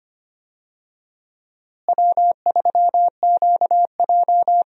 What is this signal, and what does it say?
Morse code, heard as a keyed tone, about 25 wpm, 700 Hz. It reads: W3QJ
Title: USA Call Signs - 4 Characters - Repeated 3x - 25wpm